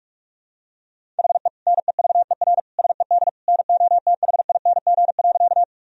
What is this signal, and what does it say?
Morse code, heard as a keyed tone, about 45 wpm, 690 Hz. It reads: HE NEVER SED NOTHING.
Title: Sentences from Top 500 Words and Abbreviations 45wpm